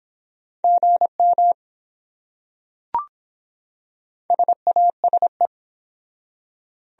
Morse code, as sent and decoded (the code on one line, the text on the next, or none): --. -- / ... .- ... .
GM SASE